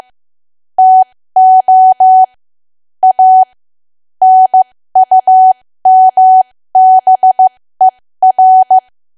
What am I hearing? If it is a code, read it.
Morse code, heard as a keyed tone, about 15 wpm, spaced as for 13 wpm, 750 Hz. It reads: TO A NUMBER
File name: cwa / W1AW-Level2 / 13WPM